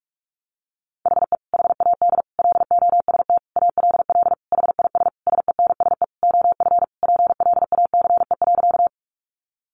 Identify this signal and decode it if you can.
Morse code, heard as a keyed tone, about 44 wpm, 700 Hz. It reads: HE HAD LOST ALL HIS SENSE OF PLACE.
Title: Sentences from Top 1400 Words 45wpm